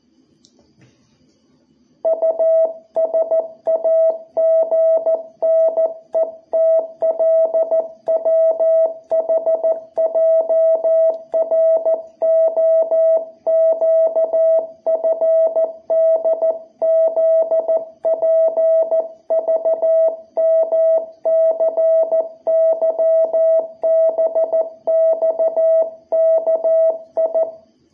Morse code, heard as a keyed tone, about 14 wpm, 660 Hz. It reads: USAGNETLWHJROQFDZPVMCYBXKI
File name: jh8cbh / 2021cw